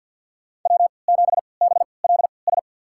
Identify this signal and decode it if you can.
Morse code, heard as a keyed tone, about 48 wpm, spaced as for 28 wpm, 700 Hz. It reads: W7BLS
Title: USA Call Signs - 5 Characters - 3x Character Spacing - 50wpm